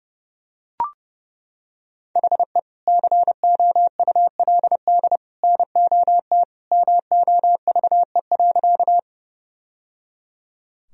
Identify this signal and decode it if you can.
Morse code, heard as a keyed tone, about 30 wpm, 700 Hz. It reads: HE COULD NOT MOVE.
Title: Sentences from Top 1000 Words - Review Part 1 - 30wpm